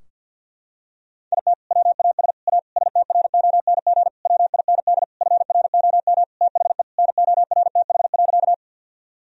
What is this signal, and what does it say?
Morse code, heard as a keyed tone, about 50 wpm, 700 Hz. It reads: IT WAS A STRONG WIND FROM THE NORTH.
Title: Sentences from Top 1000 Words - Review Part 2 - 50wpm